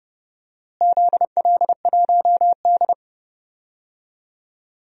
Morse code, heard as a keyed tone, about 31 wpm, 700 Hz. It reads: ZL1D
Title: DXCC-100 Call Signs - 4 Characters - Repeated 3x - 30wpm